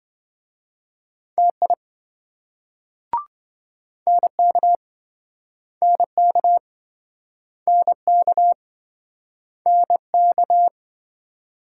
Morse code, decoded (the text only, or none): TI NK NK NK NK